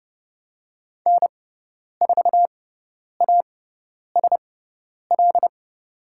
Morse code, heard as a keyed tone, about 30 wpm, spaced as for 10 wpm, 700 Hz. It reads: N4ASL
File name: Lesson 16g - Letter c - Call Signs 30wpm